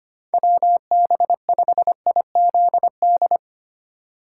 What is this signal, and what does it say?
Morse code, heard as a keyed tone, about 25 wpm, 700 Hz. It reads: WB5IZD
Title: USA Call Signs - 6 Characters - 8x Character Spacing - 25wpm